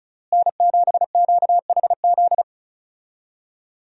Morse code, heard as a keyed tone, about 34 wpm, 700 Hz. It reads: N7QHZ
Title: USA Call Signs - 5 Characters - Repeated 2x - 35wpm